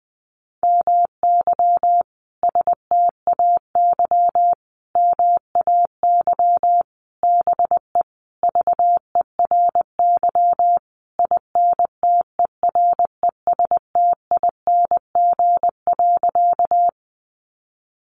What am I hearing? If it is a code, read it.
Morse code, heard as a keyed tone, about 20 wpm, 700 Hz. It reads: MY STAY MAY BE VERY INTERESTING.
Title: Sentences from Top 1000 Words 20wpm